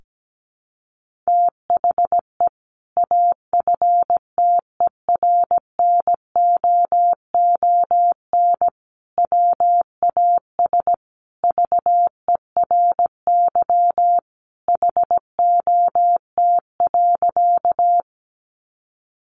Morse code, heard as a keyed tone, about 17 wpm, 700 Hz. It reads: THE AFTERNOON WAS VERY HOT.
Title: Sentences from Top 1500 Words 17wpm